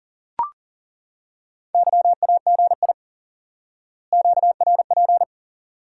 Morse code, heard as a keyed tone, about 40 wpm, 700 Hz. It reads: YAGI QRP
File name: course / instant-qso-element-course-3 / IQE Course 3 - Lesson 89 - YAGI - Two Elements - 40wpm